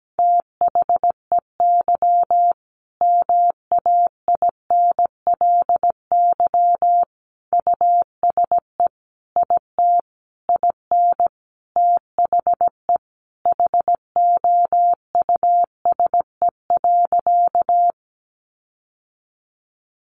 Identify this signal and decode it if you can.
Morse code, heard as a keyed tone, about 17 wpm, 700 Hz. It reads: THEY MAINLY USE IT IN THE HOUSE.
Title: Sentences from Top 1400 Words 17wpm